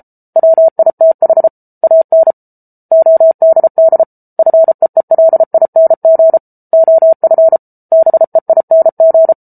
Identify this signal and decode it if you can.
Morse code, heard as a keyed tone, about 34 wpm, 650 Hz. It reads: WITH AN ODD FEELING OF BEING